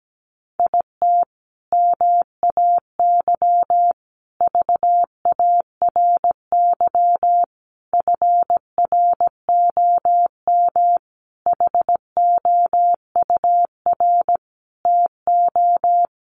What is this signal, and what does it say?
Morse code, heard as a keyed tone, about 17 wpm, 700 Hz. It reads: IT MAY VARY FROM HOUR TO
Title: Sentences from Top 1200 Words 17wpm